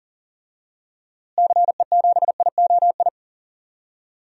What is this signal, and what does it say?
Morse code, heard as a keyed tone, about 40 wpm, 700 Hz. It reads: CE7IOI
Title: DXCC-100 Call Signs - 6 Characters - Repeated 3x - 40wpm